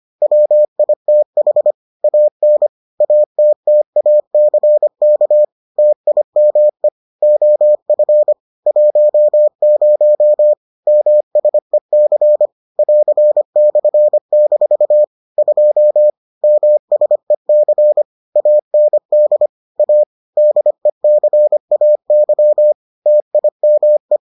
Morse code, heard as a keyed tone, about 25 wpm, 600 Hz. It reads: WITH AN ATTACK TIME OF 10 MSEC <AR>/- 2 MSEC AND A DECAY TIME